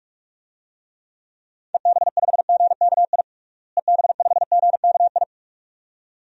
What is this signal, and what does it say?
Morse code, heard as a keyed tone, about 46 wpm, 700 Hz. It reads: EB5GKI EB5GKI
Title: DXCC-100 Call Signs - 6 Characters - Repeated 2x - 45wpm